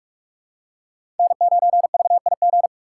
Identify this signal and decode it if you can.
Morse code, heard as a keyed tone, about 41 wpm, 700 Hz. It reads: N9VIG
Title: USA Call Signs - 5 Characters - 2x Character Spacing - 45wpm